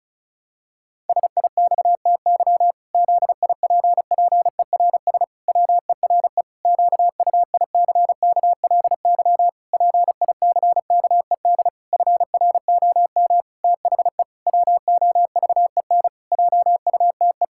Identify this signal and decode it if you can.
Morse code, heard as a keyed tone, about 34 wpm, 700 Hz. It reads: SIXTY ZIPPERS WERE QUICKLY PICKED FROM THE WOVEN JUTE
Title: Pangram Challenge v2 - 35wpm